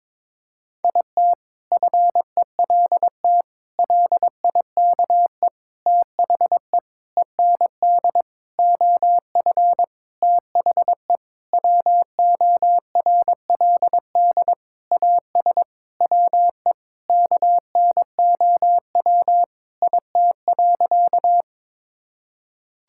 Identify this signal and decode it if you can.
Morse code, heard as a keyed tone, about 22 wpm, 700 Hz. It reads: IT FELT LIKE THE END OF THE WORLD AS WE KNOW IT.
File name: practice / 22wpm / Sentences with 8-Word Phrases - 22wpm